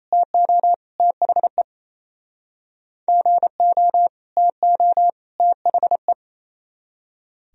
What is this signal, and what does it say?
Morse code, heard as a keyed tone, about 38 wpm, 700 Hz. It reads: TO THE GO TO THE